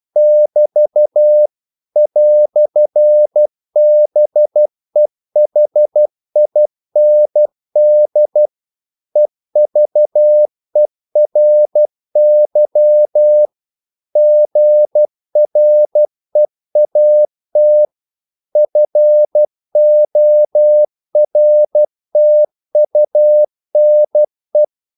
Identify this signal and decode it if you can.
Morse code, heard as a keyed tone, about 12 wpm, 600 Hz. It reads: <BT> "BEHIND EVERY GREAT FORTUNE
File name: CW / Practice / EnglishQuotations0312